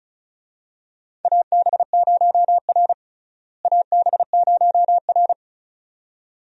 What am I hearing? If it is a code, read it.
Morse code, heard as a keyed tone, about 34 wpm, 700 Hz. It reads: AB0R AB0R
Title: USA Call Signs - 4 Characters - Repeated 2x - 35wpm